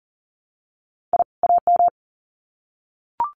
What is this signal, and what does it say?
Morse code, heard as a keyed tone, about 40 wpm, 700 Hz. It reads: I AM
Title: Sets of 2 Words 40wpm